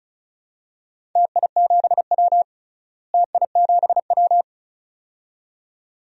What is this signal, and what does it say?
Morse code, heard as a keyed tone, about 36 wpm, 700 Hz. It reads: TI7W TI7W
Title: DXCC-100 Call Signs - 4 Characters - Repeated 2x - 35wpm